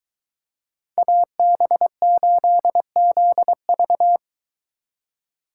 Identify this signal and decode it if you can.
Morse code, heard as a keyed tone, about 23 wpm, 700 Hz. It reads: AB8ZV